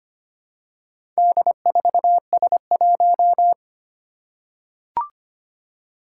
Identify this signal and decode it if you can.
Morse code, heard as a keyed tone, about 25 wpm, 700 Hz. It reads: D4S1
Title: Four Challenging Letter-Number Combinations 25wpm